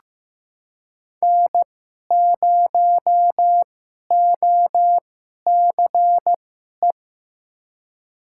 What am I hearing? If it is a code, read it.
Morse code, heard as a keyed tone, about 15 wpm, spaced as for 11 wpm, 700 Hz. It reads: N0OCE
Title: USA Call Signs - 5 Characters - 2x Character Spacing - 15wpm